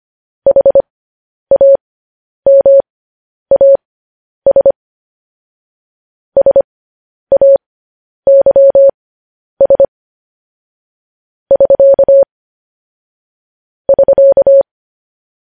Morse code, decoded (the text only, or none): HAMAS SAYS <SK> <SK>